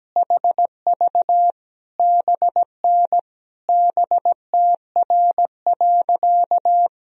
Morse code, heard as a keyed tone, about 17 wpm, 700 Hz. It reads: HV BN BTR.
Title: Sentences from Top 500 Words and Abbreviations 17wpm